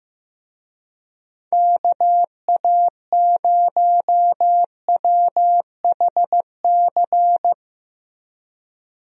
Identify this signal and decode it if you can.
Morse code, heard as a keyed tone, about 15 wpm, 700 Hz. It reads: KA0WHC
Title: USA Call Signs - 6 Characters - Repeated 2x - 15wpm